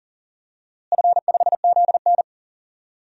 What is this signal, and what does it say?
Morse code, heard as a keyed tone, about 40 wpm, 700 Hz. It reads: F5ZN
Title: DXCC-100 Call Signs - 4 Characters - Repeated 3x - 40wpm